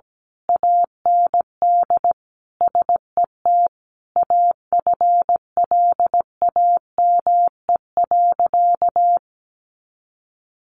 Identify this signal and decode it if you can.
Morse code, heard as a keyed tone, about 17 wpm, 700 Hz. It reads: AND SET AFLAME.
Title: A Sound Of Thunder 17wpm